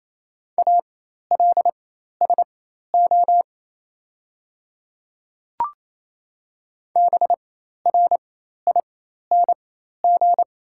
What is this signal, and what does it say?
Morse code, heard as a keyed tone, about 29 wpm, spaced as for 13 wpm, 700 Hz. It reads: ALSO BRING
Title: 4x Character Spacing - Top 500 Words - 28wpm